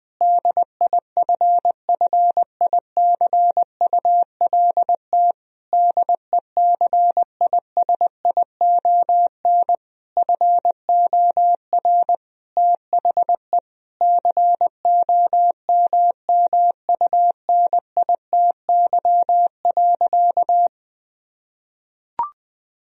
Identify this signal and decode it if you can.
Morse code, heard as a keyed tone, about 21 wpm, 700 Hz. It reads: DIFFICULT DECISION FOR THE COMMUNITY.
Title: Sentences from Top 1000 Words - Review Part 4 - 20wpm